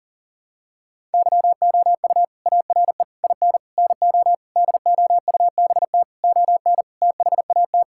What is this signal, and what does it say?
Morse code, heard as a keyed tone, about 40 wpm, 700 Hz. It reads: YOU ARE IN NO DOUBT ON THAT
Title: Sentences from Top 1600 Words 40wpm